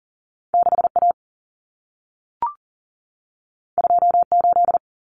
Morse code, heard as a keyed tone, about 40 wpm, 700 Hz. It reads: 6A 28
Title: Common Cut Numbers - 2-Digit 40wpm